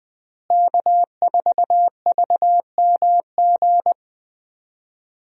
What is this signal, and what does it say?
Morse code, heard as a keyed tone, about 21 wpm, 700 Hz. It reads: K4VMG
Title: USA Call Signs - 5 Characters - Repeated 2x - 20wpm